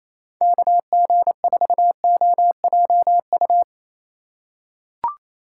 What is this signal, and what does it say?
Morse code, heard as a keyed tone, about 28 wpm, 700 Hz. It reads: KG4OJU
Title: USA Call Signs - 6 Characters - 2x Character Spacing - 28wpm